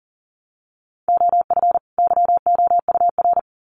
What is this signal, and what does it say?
Morse code, heard as a keyed tone, about 40 wpm, 700 Hz. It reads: OF YOUR
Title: Sets of 2 Words 40wpm